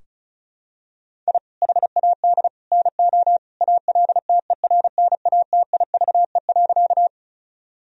Morse code, heard as a keyed tone, about 36 wpm, 700 Hz. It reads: I HAD NO ALTERNATIVE.
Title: Sentences from Top 1000 Words - Review Part 4 - 35wpm